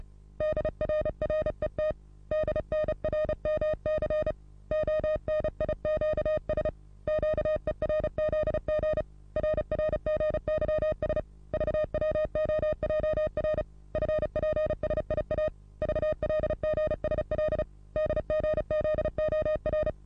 Morse code, heard as a keyed tone, about 35 wpm, 620 Hz. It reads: DRRET DNRMC ONIQS QERZG RRGYS VWOJR FPSIA VLGSL DGZOR
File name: repos / Morsekurs / DJ4UF Morsen27